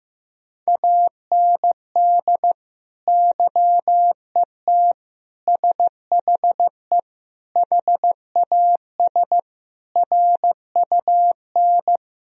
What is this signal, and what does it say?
Morse code, heard as a keyed tone, about 16 wpm, 700 Hz. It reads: AND YET SHE HAS RUN
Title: Sentences from Top 1000 Words - Review Part 1 - 15wpm